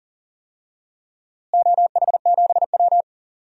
Morse code, heard as a keyed tone, about 40 wpm, 700 Hz. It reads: OH7W
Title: DXCC-100 Call Signs - 4 Characters - Repeated 2x - 40wpm